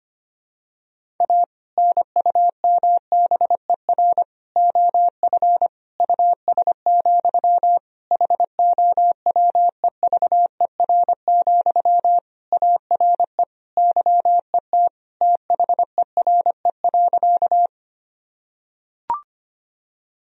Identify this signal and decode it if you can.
Morse code, heard as a keyed tone, about 25 wpm, 700 Hz. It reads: A NUMBER OF US, HOWEVER, ARE YET THERE.
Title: Sentences from Top 1000 Words - Review Part 1 - 25wpm